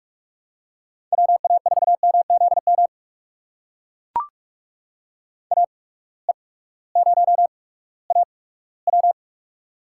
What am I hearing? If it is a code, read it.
Morse code, heard as a keyed tone, about 44 wpm, spaced as for 20 wpm, 700 Hz. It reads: WA4MZM AE0AW
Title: USA Call Signs - 8x Character Spacing - 45wpm